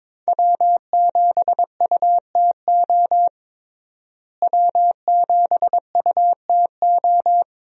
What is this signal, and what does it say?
Morse code, heard as a keyed tone, about 22 wpm, 700 Hz. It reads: W7UTO W7UTO
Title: USA Call Signs - 5 Characters - Repeated 3x - 22wpm